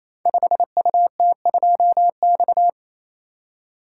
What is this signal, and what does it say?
Morse code, heard as a keyed tone, about 28 wpm, 700 Hz. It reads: 5UT2X
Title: Code Groups - 5 Letter:Numbers - 28wpm